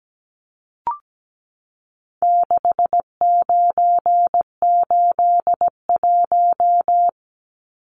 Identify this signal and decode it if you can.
Morse code, heard as a keyed tone, about 17 wpm, 700 Hz. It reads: 6981
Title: Numbers - 4-Digit 17wpm